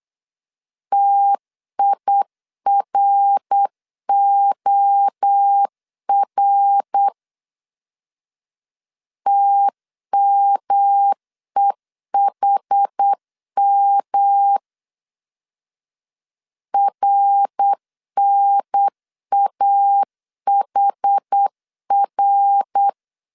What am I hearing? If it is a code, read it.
Morse code, heard as a keyed tone, about 8 wpm, 800 Hz. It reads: TIROR TMEHM RNAHR
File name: CW / DET / 1 3 1500C S B 8wpm 4141